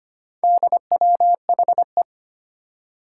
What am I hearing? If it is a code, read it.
Morse code, heard as a keyed tone, about 25 wpm, 700 Hz. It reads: DWHE
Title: Four Challenging Letter-Number Combinations 25wpm